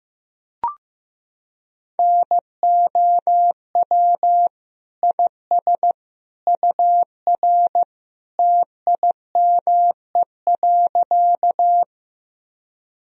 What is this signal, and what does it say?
Morse code, heard as a keyed tone, about 15 wpm, 700 Hz. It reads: NOW IS UR TIME.
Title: Sentences from Top 100 Words and Abbreviations 15wpm